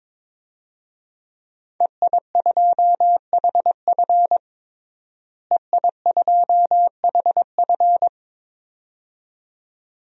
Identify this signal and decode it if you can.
Morse code, heard as a keyed tone, about 22 wpm, 700 Hz. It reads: EI2HF EI2HF
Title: DXCC-100 Call Signs - 5 Characters - Repeated 2x - 22wpm